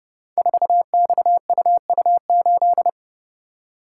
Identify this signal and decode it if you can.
Morse code, heard as a keyed tone, about 30 wpm, 700 Hz. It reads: 4XUU8
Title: Code Groups - 5 Letter:Numbers - 30wpm